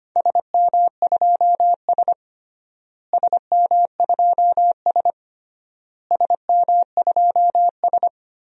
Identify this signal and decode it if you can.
Morse code, heard as a keyed tone, about 25 wpm, 700 Hz. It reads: SM2S SM2S SM2S
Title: DXCC-100 Call Signs - 4 Characters - Repeated 3x - 25wpm